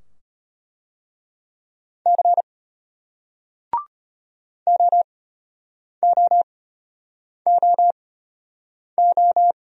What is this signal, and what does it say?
Morse code, heard as a keyed tone, about 34 wpm, spaced as for 8 wpm, 700 Hz. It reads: C OOOO